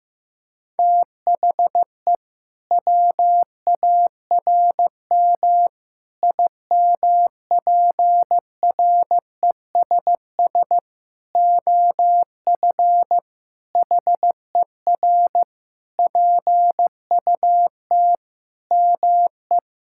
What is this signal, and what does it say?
Morse code, heard as a keyed tone, about 15 wpm, 700 Hz. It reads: THE WARM IMPRESS OF HER PUT ME
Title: Sentences from Top 1600 Words 15wpm